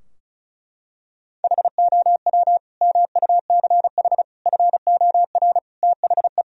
Morse code, heard as a keyed tone, about 36 wpm, 700 Hz. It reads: HOW MUCH FOR THE
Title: Sentences from Top 1000 Words - Review Part 2 - 35wpm